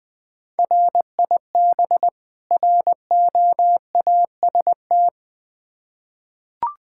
Morse code, heard as a keyed tone, about 20 wpm, 700 Hz. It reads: RIB ROAST
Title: Difficult Words with Hints - 20wpm